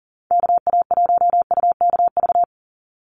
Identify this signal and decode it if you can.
Morse code, heard as a keyed tone, about 40 wpm, 690 Hz. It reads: KA1UKV
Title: USA Call Signs 40wpm